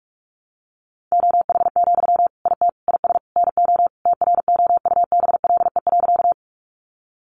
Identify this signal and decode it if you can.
Morse code, heard as a keyed tone, about 44 wpm, 700 Hz. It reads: OH, IT IS NO TROUBLE.
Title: Sentences from Top 1200 Words 45wpm